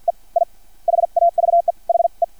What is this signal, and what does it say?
Morse code, heard as a keyed tone, about 56 wpm, spaced as for 44 wpm, 690 Hz. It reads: E I HAVE HE
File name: archived / CRSnet